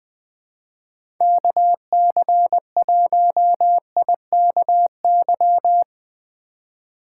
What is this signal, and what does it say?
Morse code, heard as a keyed tone, about 21 wpm, 700 Hz. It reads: KC1IKY
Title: USA Call Signs - 2x Character Spacing - 20wpm